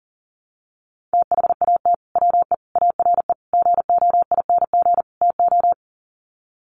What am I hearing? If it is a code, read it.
Morse code, heard as a keyed tone, about 37 wpm, 700 Hz. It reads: THAT WE ARE GOING TO
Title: Sets of 5 Words Encore 40wpm